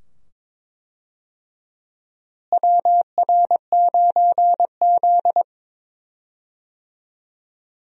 Morse code, heard as a keyed tone, about 22 wpm, 700 Hz. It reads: WR9Z
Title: USA Call Signs - 4 Characters - Repeated 2x - 22wpm